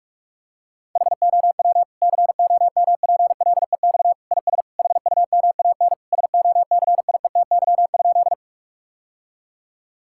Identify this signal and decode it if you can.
Morse code, heard as a keyed tone, about 46 wpm, 700 Hz. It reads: HOW COMPLEX IS HUMAN SOCIETY?
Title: Sentences from Top 1000 Words - Review Part 6 - 45wpm